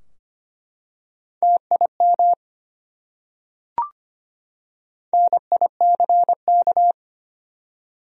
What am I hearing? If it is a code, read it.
Morse code, heard as a keyed tone, about 25 wpm, 700 Hz. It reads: TIM NICK
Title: USA Common Names - Max 4 Letters - 25wpm